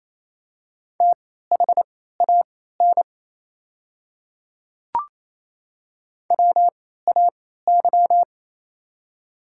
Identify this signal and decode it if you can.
Morse code, heard as a keyed tone, about 28 wpm, spaced as for 16 wpm, 700 Hz. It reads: THAN WAY
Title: 3x Character Spacing - Top 100 Words - 28wpm